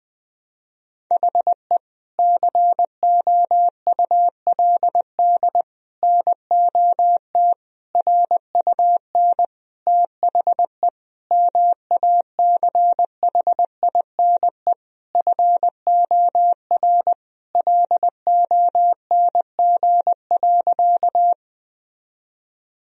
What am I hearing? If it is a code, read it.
Morse code, heard as a keyed tone, about 21 wpm, 700 Hz. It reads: HE COULD NOT RUN THE MACHINE FOR LONG.
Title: Sentences from Top 1000 Words - Review Part 2 - 20wpm